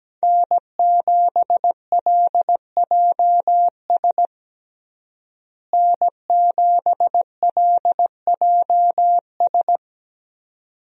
Morse code, heard as a keyed tone, about 17 wpm, 700 Hz. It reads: N7LJS N7LJS
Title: USA Call Signs - 5 Characters - Repeated 2x - 17wpm